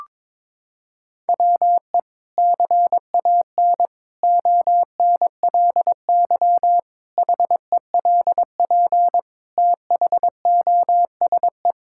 Morse code, heard as a keyed tone, about 22 wpm, 700 Hz. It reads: WE CAN ONLY HELP THOSE
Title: Sentences from Top 1000 Words - Review Part 5 - 22wpm